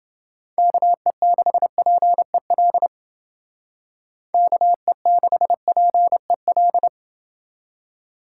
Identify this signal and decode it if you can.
Morse code, heard as a keyed tone, about 28 wpm, 700 Hz. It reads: KE6PEL KE6PEL